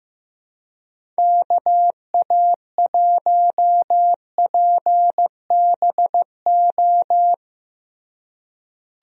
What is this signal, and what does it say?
Morse code, heard as a keyed tone, about 15 wpm, 700 Hz. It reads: KA1PBO
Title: USA Call Signs - 6 Characters - Repeated 2x - 15wpm